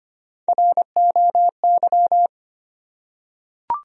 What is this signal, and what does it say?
Morse code, heard as a keyed tone, about 25 wpm, 700 Hz. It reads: ROY
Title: USA Common Names - Max 3 Letters - 25wpm